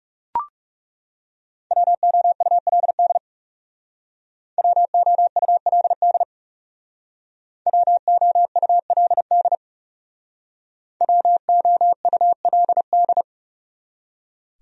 Morse code, heard as a keyed tone, about 38 wpm, 700 Hz. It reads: WOULD WOULD WOULD WOULD